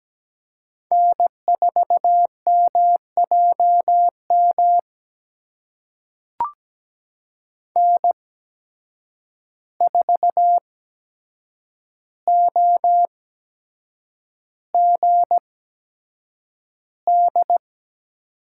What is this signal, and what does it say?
Morse code, heard as a keyed tone, about 17 wpm, spaced as for 7 wpm, 700 Hz. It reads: N4MJM N4OGD
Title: USA Call Signs - 5 Characters - 8x Character Spacing - 17wpm